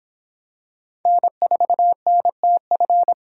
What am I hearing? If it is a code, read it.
Morse code, heard as a keyed tone, about 26 wpm, 700 Hz. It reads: N4NTF